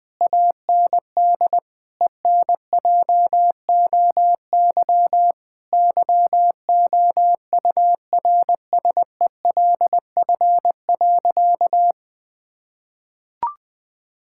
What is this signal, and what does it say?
Morse code, heard as a keyed tone, about 21 wpm, 700 Hz. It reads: AND ENJOY YOURSELF.
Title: Sentences from Top 1000 Words - Review Part 5 - 20wpm